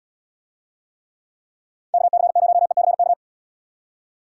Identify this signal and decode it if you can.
Morse code, heard as a keyed tone, about 77 wpm, 700 Hz. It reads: KC0ECK